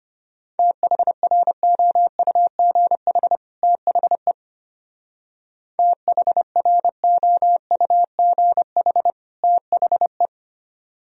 Morse code, heard as a keyed tone, about 32 wpm, 700 Hz. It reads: THROUGH THE THROUGH THE